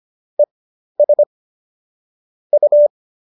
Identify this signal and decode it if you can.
Morse code, heard as a keyed tone, about 25 wpm, spaced as for 12 wpm, 590 Hz. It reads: ES U
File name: qso012 12